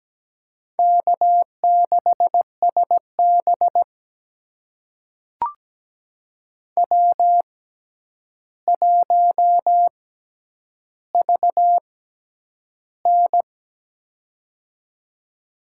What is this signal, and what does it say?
Morse code, heard as a keyed tone, about 17 wpm, spaced as for 9 wpm, 700 Hz. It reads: K6SB W1VN
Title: USA Call Signs - 4 Characters - 6x Character Spacing - 17wpm